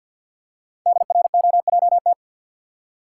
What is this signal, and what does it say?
Morse code, heard as a keyed tone, about 50 wpm, 700 Hz. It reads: DROJT
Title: Code Groups - 5 Letters - 50wpm